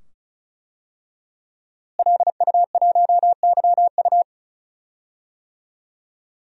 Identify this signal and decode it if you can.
Morse code, heard as a keyed tone, about 36 wpm, 700 Hz. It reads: LU1YU
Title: DXCC-100 Call Signs - 5 Characters - Repeated 2x - 35wpm